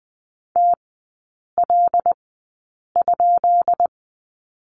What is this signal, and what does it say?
Morse code, heard as a keyed tone, about 20 wpm, spaced as for 8 wpm, 690 Hz. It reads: TL?
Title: 133 Lesson 22e - Question Mark - Three Characters 20wpm